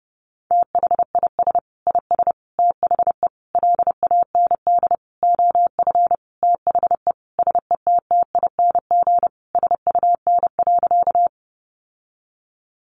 Morse code, heard as a keyed tone, about 30 wpm, 700 Hz. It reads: THIS IS THE LAND OF THE SETTING SUN.
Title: Sentences from Top 1300 Words 30wpm